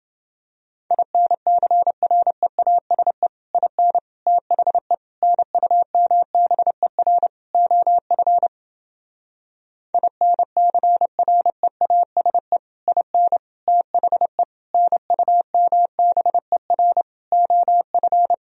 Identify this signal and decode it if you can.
Morse code, heard as a keyed tone, about 29 wpm, 700 Hz. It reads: INCREASE IN THE NUMBER OF INCREASE IN THE NUMBER OF